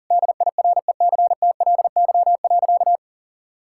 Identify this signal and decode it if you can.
Morse code, heard as a keyed tone, about 40 wpm, 700 Hz. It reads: DIRECTLY.
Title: Sentences from Top 1000 Words - Review Part 5 - 40wpm